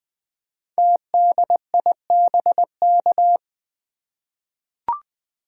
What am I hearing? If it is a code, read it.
Morse code, heard as a keyed tone, about 20 wpm, 700 Hz. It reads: TDIBK
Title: Code Groups - 5 Letters - 20wpm